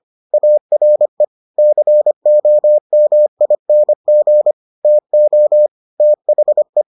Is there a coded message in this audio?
Yes: Morse code from 0.3 to 6.8 s, heard as a keyed tone, about 25 wpm, 600 Hz. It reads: ARE COMING TO THE